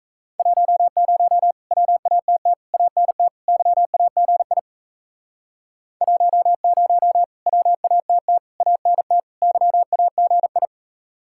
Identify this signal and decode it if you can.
Morse code, heard as a keyed tone, about 37 wpm, 700 Hz. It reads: 10 WATT ANT YAGI 10 WATT ANT YAGI